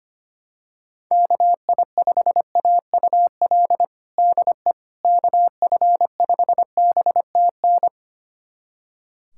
Morse code, heard as a keyed tone, about 25 wpm, 700 Hz. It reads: KI5AUL DE KF5BTN
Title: USA Callsign DE Callsign - 6 Characters - 25wpm